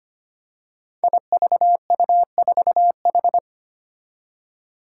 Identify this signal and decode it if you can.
Morse code, heard as a keyed tone, about 25 wpm, 700 Hz. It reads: IVU4H
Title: Five Challenging Letter-Number Combinations 25wpm